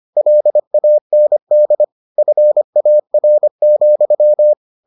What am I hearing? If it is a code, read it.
Morse code, heard as a keyed tone, about 25 wpm, 600 Hz. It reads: LAND FAR,